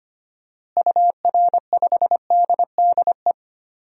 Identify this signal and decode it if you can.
Morse code, heard as a keyed tone, about 25 wpm, 700 Hz. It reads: UR5DDE
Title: DXCC-100 Call Signs - 6 Characters - Repeated 3x - 25wpm